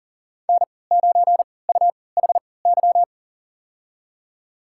Morse code, heard as a keyed tone, about 40 wpm, spaced as for 23 wpm, 700 Hz. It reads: N9UHY
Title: USA Call Signs - 5 Characters - 3x Character Spacing - 40wpm